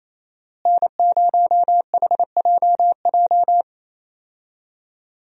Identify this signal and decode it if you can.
Morse code, heard as a keyed tone, about 27 wpm, 700 Hz. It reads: N0HJJ
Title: USA Call Signs - 5 Characters - Repeated 2x - 28wpm